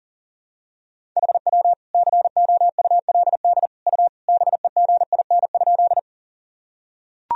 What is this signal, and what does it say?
Morse code, heard as a keyed tone, about 40 wpm, 700 Hz. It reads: HW COULD U BEGIN?
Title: Sentences from Top 300 Words and Abbreviations 40wpm